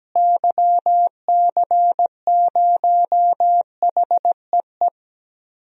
Morse code, heard as a keyed tone, about 17 wpm, 700 Hz. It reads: YC0HEE
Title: DXCC-100 Call Signs - 6 Characters - Repeated 2x - 17wpm